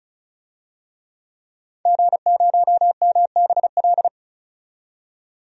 Morse code, heard as a keyed tone, about 36 wpm, 700 Hz. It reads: G0MBL
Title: DXCC-100 Call Signs - 5 Characters - 35wpm